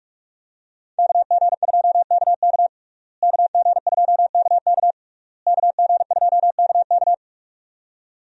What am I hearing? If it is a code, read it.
Morse code, heard as a keyed tone, about 41 wpm, 700 Hz. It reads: KG2KK KG2KK KG2KK
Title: USA Call Signs - 5 Characters - Repeated 3x - 45wpm